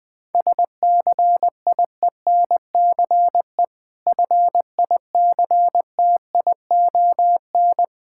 Morse code, heard as a keyed tone, about 20 wpm, 690 Hz. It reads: SCIENCE FICTION
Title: Difficult Words with Hints - 20wpm